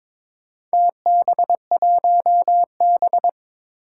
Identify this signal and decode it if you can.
Morse code, heard as a keyed tone, about 22 wpm, 700 Hz. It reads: TB1B
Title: Four Challenging Letter-Number Combinations 22wpm